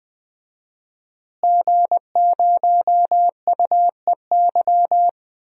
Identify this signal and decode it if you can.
Morse code, heard as a keyed tone, about 20 wpm, 700 Hz. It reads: G0UEY
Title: DXCC-100 Call Signs - 5 Characters - Repeated 2x - 20wpm